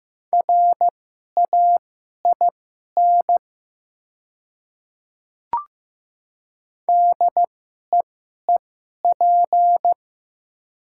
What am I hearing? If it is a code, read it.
Morse code, heard as a keyed tone, about 16 wpm, spaced as for 11 wpm, 700 Hz. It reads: RAIN DEEP
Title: 2x Character Spacing - Top 500 Words - 15wpm